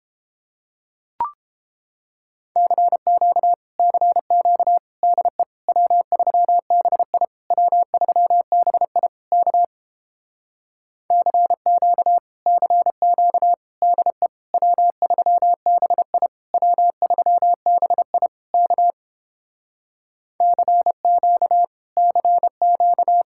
Morse code, heard as a keyed tone, about 29 wpm, 700 Hz. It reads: CQ CQ DE W3BI W3BI K CQ CQ DE W3BI W3BI K CQ CQ